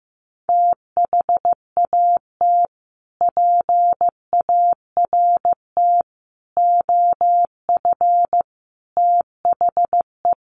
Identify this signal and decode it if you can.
Morse code, heard as a keyed tone, about 15 wpm, 700 Hz. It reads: THAT PART OF THE
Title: Sets of 5 Words Encore2 15wpm